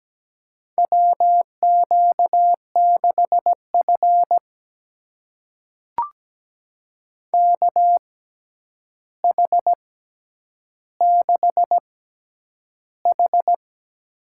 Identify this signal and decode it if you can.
Morse code, heard as a keyed tone, about 17 wpm, spaced as for 9 wpm, 700 Hz. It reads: WQ6F KH6H
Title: USA Call Signs - 4 Characters - 6x Character Spacing - 17wpm